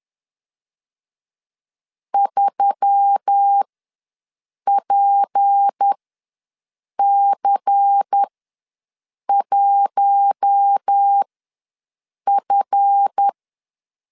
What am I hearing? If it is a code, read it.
Morse code, heard as a keyed tone, about 10 wpm, spaced as for 6 wpm, 800 Hz. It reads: 3PC1F
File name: LEZIONI / 8 YJ45 / 8 YJ ws 10wpm 4043